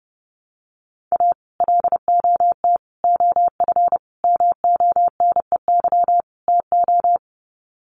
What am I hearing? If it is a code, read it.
Morse code, heard as a keyed tone, about 30 wpm, 700 Hz. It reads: A LOT OF MONEY TO
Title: Sets of 5 Words Encore2 30wpm